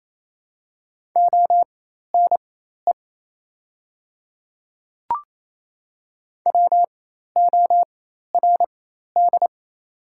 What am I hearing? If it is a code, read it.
Morse code, heard as a keyed tone, about 28 wpm, spaced as for 13 wpm, 700 Hz. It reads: ONE WORD
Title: 4x Character Spacing - Top 100 Words - 28wpm